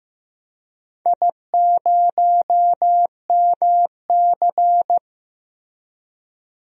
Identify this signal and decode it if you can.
Morse code, heard as a keyed tone, about 15 wpm, 700 Hz. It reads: I0MC
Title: DXCC-100 Call Signs - 4 Characters - Repeated 3x - 15wpm